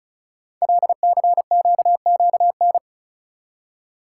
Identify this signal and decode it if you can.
Morse code, heard as a keyed tone, about 36 wpm, 690 Hz. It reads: LCQQN
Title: Code Groups - 5 Letters - 35wpm